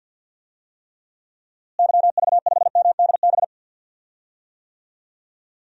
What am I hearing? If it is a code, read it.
Morse code, heard as a keyed tone, about 53 wpm, 700 Hz. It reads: YV5MDB
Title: DXCC-100 Call Signs - 6 Characters - Repeated 2x - 50wpm